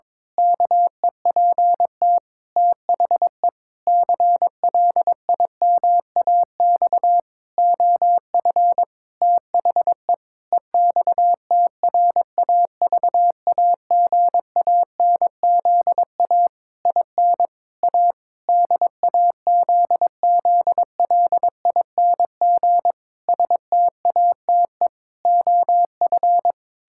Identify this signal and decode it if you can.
Morse code, heard as a keyed tone, about 22 wpm, 700 Hz. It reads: KEPT THE CLIMAX OF THE EXTRAVAGANZA IN A DAZZLING STATE OF
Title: Pangram Challenge v2 - 22wpm